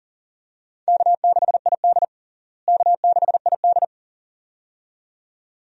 Morse code, heard as a keyed tone, about 40 wpm, 700 Hz. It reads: K6ID K6ID
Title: USA Call Signs - 4 Characters - Repeated 2x - 40wpm